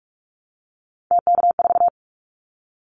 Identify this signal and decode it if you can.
Morse code, heard as a keyed tone, about 44 wpm, 700 Hz. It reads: TK4
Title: Callsign Prefixes for Top 100 DXCC - 45wpm